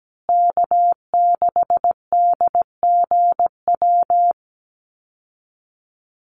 Morse code, heard as a keyed tone, about 17 wpm, 700 Hz. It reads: K6DGW
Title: Call Signs Repeated 3x - 17wpm